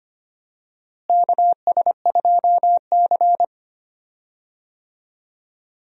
Morse code, heard as a keyed tone, about 25 wpm, 700 Hz. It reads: KS2C
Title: USA Call Signs - 4 Characters - Repeated 2x - 25wpm